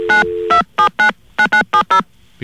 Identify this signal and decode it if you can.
DTMF, heard as telephone touch tones, tones of about 100 ms. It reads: 92799970